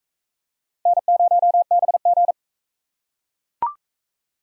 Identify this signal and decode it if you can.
Morse code, heard as a keyed tone, about 41 wpm, 700 Hz. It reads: N0BG